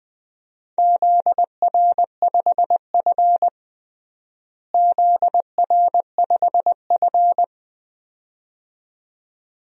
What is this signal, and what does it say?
Morse code, heard as a keyed tone, about 20 wpm, 700 Hz. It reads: ZR5F ZR5F
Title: DXCC-100 Call Signs - 4 Characters - Repeated 2x - 20wpm